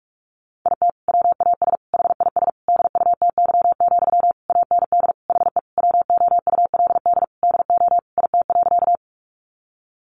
Morse code, heard as a keyed tone, about 44 wpm, 700 Hz. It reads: IT WAS HIS DUTY, AND HE WOULD DO IT.
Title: Sentences from Top 1500 Words 45wpm